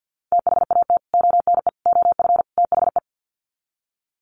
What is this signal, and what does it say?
Morse code, heard as a keyed tone, about 50 wpm, 700 Hz. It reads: THAT ONE OF THE
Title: Sets of 4 Words 50wpm